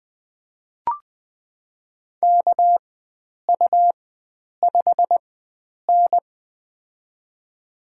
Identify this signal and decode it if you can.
Morse code, heard as a keyed tone, about 21 wpm, spaced as for 9 wpm, 700 Hz. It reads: KU5N
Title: USA Call Signs - 4x Character Spacing - 20wpm